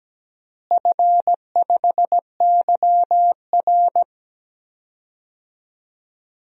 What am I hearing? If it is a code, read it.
Morse code, heard as a keyed tone, about 17 wpm, 700 Hz. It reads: F5YR
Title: DXCC-100 Call Signs - 4 Characters - Repeated 2x - 17wpm